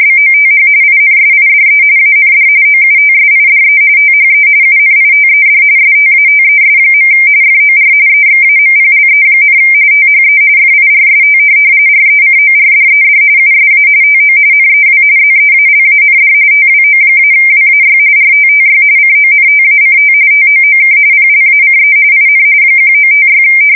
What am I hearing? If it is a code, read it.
RTTY, about 45.45 baud. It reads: BOTH INSIDE AND OUTSIDE THE UNITED STATES. FOR EXAMPLE, THE FIRST RECORDED USA TO NEW ZEALAND TWO-WAY RTTY CONTACT TOOK PLACE IN 1956
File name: fsk170 / teletype-RTTY